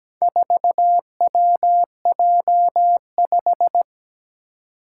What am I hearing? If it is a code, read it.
Morse code, heard as a keyed tone, about 17 wpm, 700 Hz. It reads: 4WJ5
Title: Four Challenging Letter-Number Combinations 17wpm